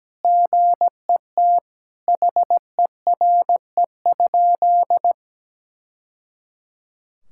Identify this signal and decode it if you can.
Morse code, heard as a keyed tone, about 17 wpm, 700 Hz. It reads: GET HERE?
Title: Sentences from Top 1000 Words - Review Part 6 - 17wpm